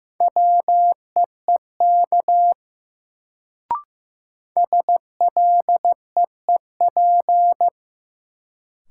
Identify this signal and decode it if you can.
Morse code, heard as a keyed tone, about 15 wpm, 700 Hz. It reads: WEEK SLEEP
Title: Sentences from Top 500 Words - New Word Drill - 15wpm